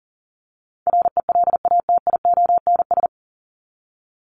Morse code, heard as a keyed tone, about 40 wpm, 700 Hz. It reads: RELATIONS
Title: ES:S Suffix - Top 100 Words with Base Words 40wpm